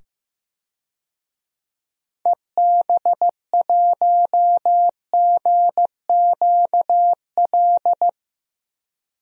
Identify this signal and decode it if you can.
Morse code, heard as a keyed tone, about 15 wpm, 700 Hz. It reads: EB1GQL
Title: DXCC-100 Call Signs - 6 Characters - 15wpm